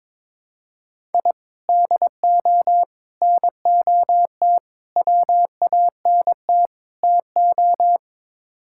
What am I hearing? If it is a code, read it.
Morse code, heard as a keyed tone, about 22 wpm, 700 Hz. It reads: I DO NOT WANT TO
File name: practice / 22wpm / Build A Phrase - 5-Word Phrases - 22wpm